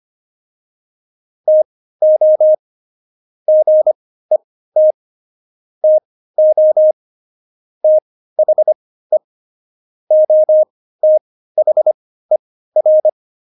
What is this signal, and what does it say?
Morse code, heard as a keyed tone, about 25 wpm, spaced as for 15 wpm, 620 Hz. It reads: TO GET TO THE OTHER
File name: Jokes-25-char-15-eff